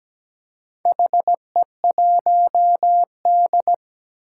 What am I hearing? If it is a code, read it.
Morse code, heard as a keyed tone, about 17 wpm, 700 Hz. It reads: HE1D
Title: Four Challenging Letter-Number Combinations 17wpm